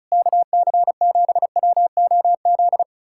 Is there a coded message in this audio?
Yes: Morse code from 0.1 to 2.8 s, heard as a keyed tone, about 40 wpm, 700 Hz. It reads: KC7WOZ